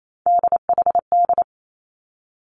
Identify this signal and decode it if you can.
Morse code, heard as a keyed tone, about 28 wpm, 700 Hz. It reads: DHD
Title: Three Challenging Letter-Number Combinations 28wpm